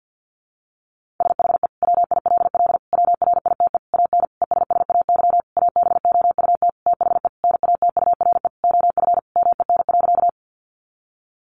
Morse code, heard as a keyed tone, about 50 wpm, 700 Hz. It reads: SHE WILL WRITE AN ESSAY ABOUT THE NATURE OF MEN.
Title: Sentences from Top 1500 Words 50wpm